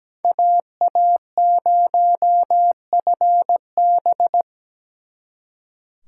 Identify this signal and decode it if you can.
Morse code, heard as a keyed tone, about 20 wpm, 700 Hz. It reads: AA0FB
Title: Speed Racing - USA 2x2 Call Signs 17wpm